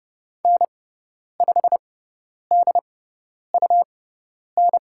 Morse code, heard as a keyed tone, about 30 wpm, spaced as for 10 wpm, 700 Hz. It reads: N5DUN
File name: Lesson 17g - Letter u - Call Signs 30wpm